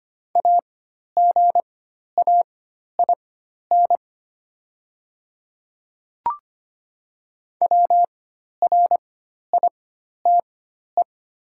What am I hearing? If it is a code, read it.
Morse code, heard as a keyed tone, about 25 wpm, spaced as for 12 wpm, 700 Hz. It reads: AGAIN WRITE
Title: 4x Character Spacing - Top 500 Words - 25wpm